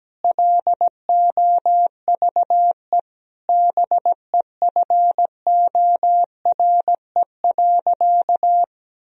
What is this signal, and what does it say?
Morse code, heard as a keyed tone, about 17 wpm, 700 Hz. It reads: LOVE BEFORE.
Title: Sentences from Top 1000 Words - Review Part 3 - 17wpm